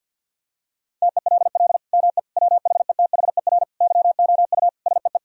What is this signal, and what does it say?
Morse code, heard as a keyed tone, about 50 wpm, 700 Hz. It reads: TELL ME WHETHER YOU SEE
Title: Sentences from Top 1000 Words - Review Part 3 - 50wpm